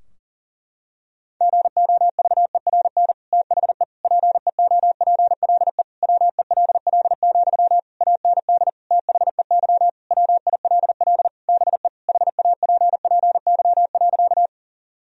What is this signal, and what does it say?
Morse code, heard as a keyed tone, about 40 wpm, 700 Hz. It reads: GOVERN THE PEOPLE WELL, AND THEY WILL BE HAPPY.
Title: Sentences from Top 1000 Words - Review Part 2 - 40wpm